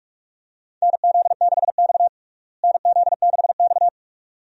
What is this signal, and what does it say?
Morse code, heard as a keyed tone, about 46 wpm, 700 Hz. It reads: NZ6X NZ6X
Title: USA Call Signs - 4 Characters - Repeated 2x - 45wpm